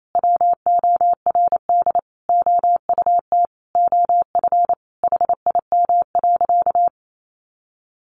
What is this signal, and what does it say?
Morse code, heard as a keyed tone, about 28 wpm, 700 Hz. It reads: WORD OUT OF HIM.
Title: Sentences from Top 1500 Words 28wpm